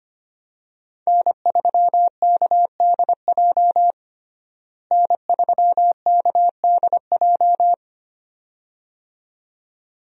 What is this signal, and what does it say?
Morse code, heard as a keyed tone, about 25 wpm, 700 Hz. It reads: N3KDJ N3KDJ
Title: USA Call Signs - 5 Characters - Repeated 2x - 25wpm